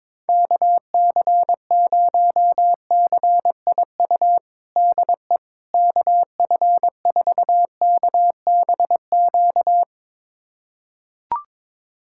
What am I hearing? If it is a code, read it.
Morse code, heard as a keyed tone, about 21 wpm, 700 Hz. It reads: KC0CIU DE KF4KBQ
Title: USA Callsign DE Callsign - 6 Characters - 22wpm